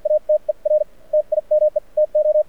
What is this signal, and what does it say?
Morse code, heard as a keyed tone, about 53 wpm, spaced as for 42 wpm, 660 Hz, with radio noise. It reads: ATER TIME TO
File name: archived / CRSnet